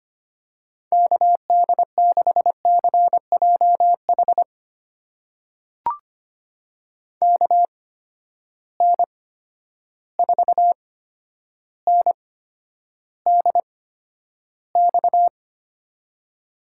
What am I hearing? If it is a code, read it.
Morse code, heard as a keyed tone, about 25 wpm, spaced as for 11 wpm, 700 Hz. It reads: KD6CJH KN4NDX
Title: USA Call Signs - 8x Character Spacing - 25wpm